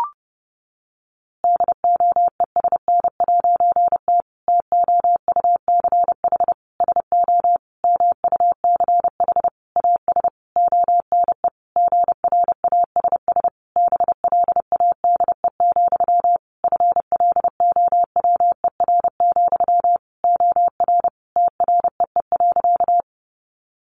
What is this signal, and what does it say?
Morse code, heard as a keyed tone, about 30 wpm, 700 Hz. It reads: DOESN'T TOUCH SO MUCH AS ONE GRASS BLADE, FLOWER, OR TREE.
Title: A Sound Of Thunder 30wpm